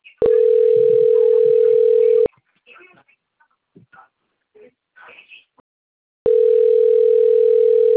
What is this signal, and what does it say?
Ringback tone (North America), 440 + 480 Hz, with speech.
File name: rg-1600-unknown- - - .63479